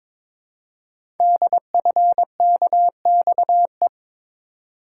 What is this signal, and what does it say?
Morse code, heard as a keyed tone, about 22 wpm, 700 Hz. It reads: DFKXE
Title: Code Groups - 5 Letters - 22wpm